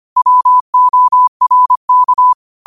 Morse code, heard as a keyed tone, about 25 wpm, 1000 Hz. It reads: WORK